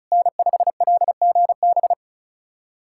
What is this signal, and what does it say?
Morse code, heard as a keyed tone, about 36 wpm, 700 Hz. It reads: N5LGB
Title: USA Callsign DE Callsign - 5 Characters - 35wpm